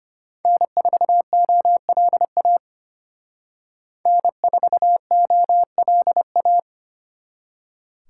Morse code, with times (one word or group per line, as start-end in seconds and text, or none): 0.5-2.6 N4OLA
4.1-6.6 N4OLA